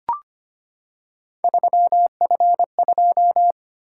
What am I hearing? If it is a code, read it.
Morse code, heard as a keyed tone, about 25 wpm, 700 Hz. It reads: 3F2
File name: Three Letter-Number Combinations - 25wpm